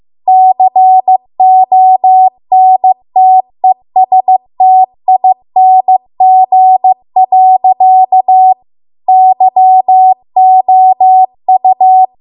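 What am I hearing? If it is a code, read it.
Morse code, heard as a keyed tone, about 15 wpm, 750 Hz. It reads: CONTESTING. YOU
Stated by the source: Morse code practice at 15 wpm from the national amateur radio association's code archive.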